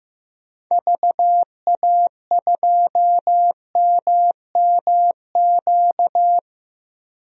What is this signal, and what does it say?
Morse code, heard as a keyed tone, about 15 wpm, 700 Hz. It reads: VA2MMQ
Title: DXCC-100 Call Signs - 6 Characters - 15wpm